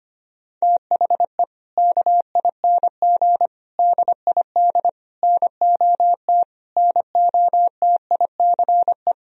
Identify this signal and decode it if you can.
Morse code, heard as a keyed tone, about 25 wpm, 700 Hz. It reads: THE KING DID NOT NOTICE
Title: Sentences from Top 1000 Words - Review Part 2 - 25wpm